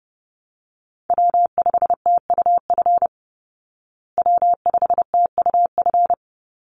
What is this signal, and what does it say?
Morse code, heard as a keyed tone, about 30 wpm, 700 Hz. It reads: W5TUF W5TUF
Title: Call Signs Repeated 3x - 30wpm